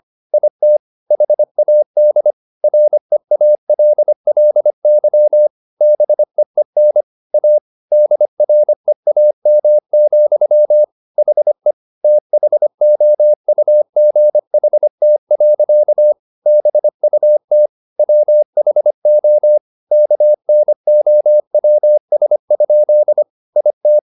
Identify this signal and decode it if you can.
Morse code, heard as a keyed tone, about 25 wpm, 600 Hz. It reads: IT HAD REALLY BEEN A DREAM, HE THOUGHT. BUT WHO KNOWS? IT